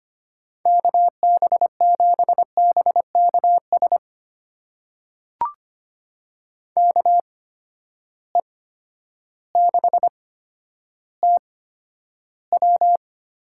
Morse code, heard as a keyed tone, about 25 wpm, spaced as for 11 wpm, 700 Hz. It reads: KB7BKS KE6TW
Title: USA Call Signs - 8x Character Spacing - 25wpm